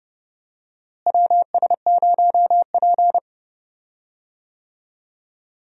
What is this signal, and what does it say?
Morse code, heard as a keyed tone, about 31 wpm, 700 Hz. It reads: WS0P
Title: USA Call Signs - 4 Characters - Repeated 3x - 30wpm